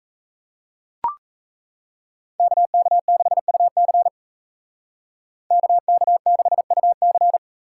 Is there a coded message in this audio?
Yes: Morse code from 2.4 to 7.4 s, heard as a keyed tone, about 36 wpm, 700 Hz. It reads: KK6UC KK6UC